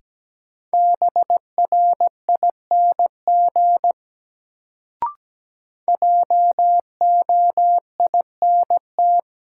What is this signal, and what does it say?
Morse code, heard as a keyed tone, about 17 wpm, 700 Hz. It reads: BRING JOINT
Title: Sets of 5 Words Encore - New Word Drill - 17wpm